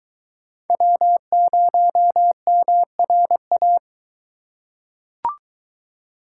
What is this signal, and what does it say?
Morse code, heard as a keyed tone, about 23 wpm, 700 Hz. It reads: W0MRA